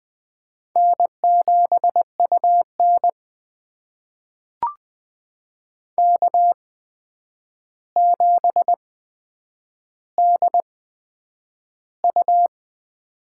Morse code, heard as a keyed tone, about 20 wpm, spaced as for 9 wpm, 700 Hz. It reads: N7UN K7DU
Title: USA Call Signs - 4 Characters - 8x Character Spacing - 20wpm